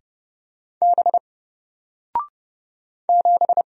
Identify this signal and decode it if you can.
Morse code, heard as a keyed tone, about 30 wpm, 700 Hz. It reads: B 7
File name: Single-Letter-Number-30wpm